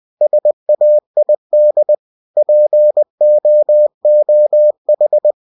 Morse code, heard as a keyed tone, about 20 wpm, 600 Hz. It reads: SAID POOH
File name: cw-books / winnie-the-pooh / winnie-20wpm-page-0077